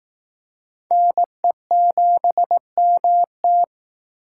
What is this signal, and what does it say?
Morse code, heard as a keyed tone, about 20 wpm, 700 Hz. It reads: NE7MT